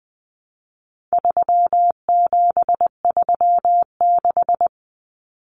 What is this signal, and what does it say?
Morse code, heard as a keyed tone, about 20 wpm, 700 Hz. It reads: 3736
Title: Numbers - 4-Digit 20wpm